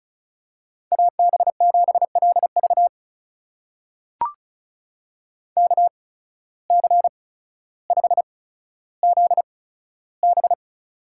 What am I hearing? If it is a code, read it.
Morse code, heard as a keyed tone, about 34 wpm, spaced as for 15 wpm, 700 Hz. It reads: AB7LV KC5ZB
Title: USA Call Signs - 5 Characters - 8x Character Spacing - 35wpm